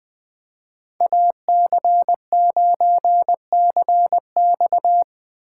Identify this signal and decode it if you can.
Morse code, heard as a keyed tone, about 24 wpm, 700 Hz. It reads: AC9CX
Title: Speed Racing - USA 2x2 Call Signs 20wpm